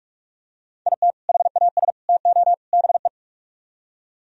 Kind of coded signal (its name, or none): Morse code